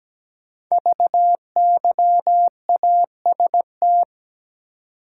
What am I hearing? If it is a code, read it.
Morse code, heard as a keyed tone, about 17 wpm, 700 Hz. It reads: VYAST
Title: Code Groups - 5 Letters - 17wpm